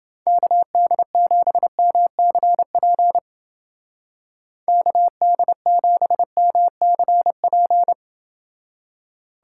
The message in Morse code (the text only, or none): KD7MCP KD7MCP